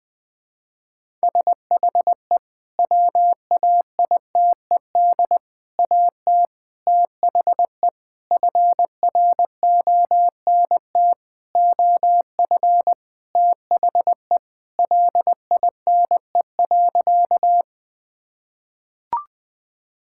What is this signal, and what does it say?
Morse code, heard as a keyed tone, about 20 wpm, 700 Hz. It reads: SHE WAITED AT THE FRONT OF THE LINE.
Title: Sentences with 5-Word Phrases - 20wpm